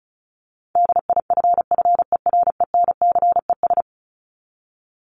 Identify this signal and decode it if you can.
Morse code, heard as a keyed tone, about 34 wpm, 700 Hz. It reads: DIFFERENCES
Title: ES:S Suffix - Top 100 Words with Base Words 35wpm